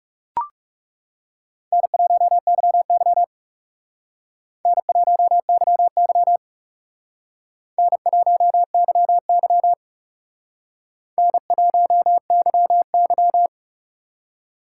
Morse code, heard as a keyed tone, about 36 wpm, 700 Hz. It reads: N1YY N1YY N1YY N1YY
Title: Speed Racing - USA 1x2 Call Signs 30wpm